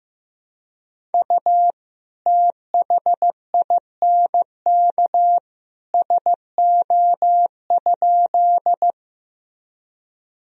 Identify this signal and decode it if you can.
Morse code, heard as a keyed tone, about 15 wpm, 700 Hz. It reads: U THINK SO?
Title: Sentences from Top 200 Words and Abbreviations 15wpm